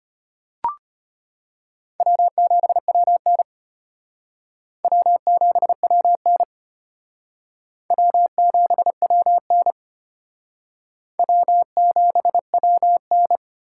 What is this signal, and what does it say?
Morse code, heard as a keyed tone, about 32 wpm, 700 Hz. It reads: W7WN W7WN W7WN W7WN